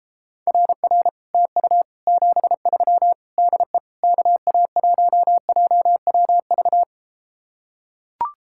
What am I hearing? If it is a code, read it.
Morse code, heard as a keyed tone, about 33 wpm, 700 Hz. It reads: RR TU 73 DE KA1JWV